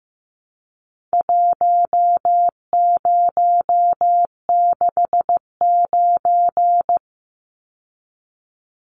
Morse code, heard as a keyed tone, about 15 wpm, 700 Hz. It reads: 1069
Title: Numbers - Contest Style - 4-Digit 15wpm